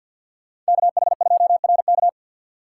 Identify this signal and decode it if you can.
Morse code, heard as a keyed tone, about 64 wpm, 700 Hz. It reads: KH2RK